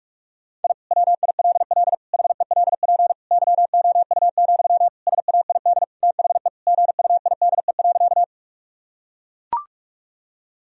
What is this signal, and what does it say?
Morse code, heard as a keyed tone, about 41 wpm, 700 Hz. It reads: I WILL HELP YOU, SAID THE GUIDE.
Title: Sentences from Top 1000 Words - Review Part 3 - 45wpm